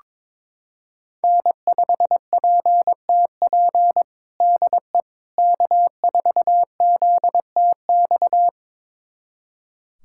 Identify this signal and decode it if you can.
Morse code, heard as a keyed tone, about 21 wpm, 700 Hz. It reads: N5PTP DE K4ZTX
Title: USA Callsign DE Callsign - 5 Characters - 22wpm